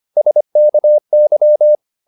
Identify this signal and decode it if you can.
Morse code, heard as a keyed tone, about 25 wpm, 600 Hz. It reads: SKY